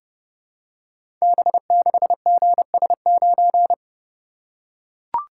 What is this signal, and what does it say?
Morse code, heard as a keyed tone, about 30 wpm, 700 Hz. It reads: B6GS9
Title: Code Groups - 5 Letter:Numbers - 30wpm